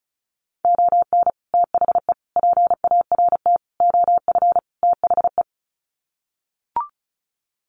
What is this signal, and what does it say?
Morse code, heard as a keyed tone, about 34 wpm, 700 Hz. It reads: ON THE PART OF THE
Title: Sets of 5 Words 35wpm